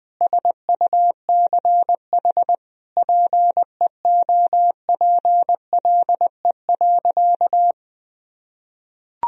Morse code, heard as a keyed tone, about 21 wpm, 700 Hz. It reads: SUCH PEOPLE.
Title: Sentences from Top 1000 Words - Review Part 3 - 20wpm